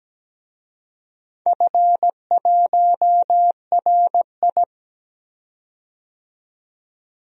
Morse code, heard as a keyed tone, about 17 wpm, 700 Hz. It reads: F1RI
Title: DXCC-100 Call Signs - 4 Characters - 17wpm